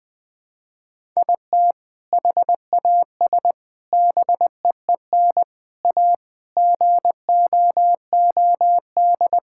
Morse code, heard as a keyed tone, about 21 wpm, 700 Hz. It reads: IT HAS BEEN A GOOD
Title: Sentences with 2-Word Phrases - Spoken Beginning - 20wpm